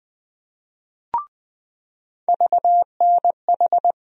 Morse code, heard as a keyed tone, about 20 wpm, 700 Hz. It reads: VNH
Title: USA Call Sign Postfixes - 3 Letters 20wpm